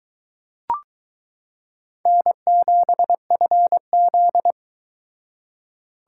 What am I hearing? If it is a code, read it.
Morse code, heard as a keyed tone, about 22 wpm, 700 Hz. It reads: N7FZ